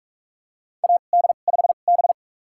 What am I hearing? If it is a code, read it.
Morse code, heard as a keyed tone, about 44 wpm, spaced as for 32 wpm, 690 Hz. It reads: AD5B
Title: USA Call Signs - 4 Characters - 2x Character Spacing - 45wpm